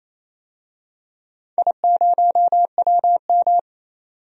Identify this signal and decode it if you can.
Morse code, heard as a keyed tone, about 29 wpm, 700 Hz. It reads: I0WM
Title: DXCC-100 Call Signs - 4 Characters - 28wpm